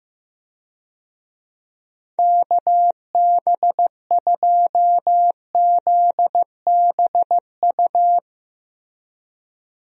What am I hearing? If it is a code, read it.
Morse code, heard as a keyed tone, about 15 wpm, 700 Hz. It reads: KB2ZBU
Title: USA Call Signs - 6 Characters - Repeated 3x - 15wpm